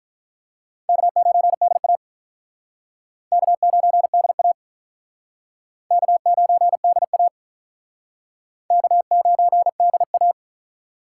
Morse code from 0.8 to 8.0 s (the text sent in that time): K9DA K9DA K9DA